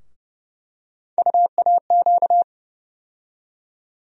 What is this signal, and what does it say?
Morse code, heard as a keyed tone, about 30 wpm, 700 Hz. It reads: UAQ
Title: USA Call Sign Postfixes - 3 Letters 30wpm